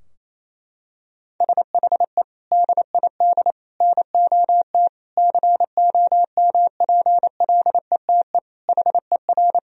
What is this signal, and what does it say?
Morse code, heard as a keyed tone, about 28 wpm, 700 Hz. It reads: SHE DID NOT COMPLETE HER
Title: Sentences from Top 1000 Words - Review Part 2 - 28wpm